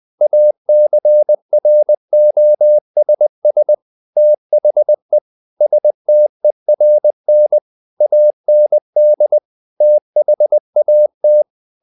Morse code, heard as a keyed tone, about 20 wpm, 600 Hz. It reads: ACROSS THE STERN AND THAT
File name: the old man and the sea part 2 20wpm